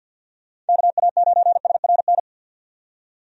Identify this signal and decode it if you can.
Morse code, heard as a keyed tone, about 51 wpm, 700 Hz. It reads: KA9SRN
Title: USA Call Signs - 6 Characters - 50wpm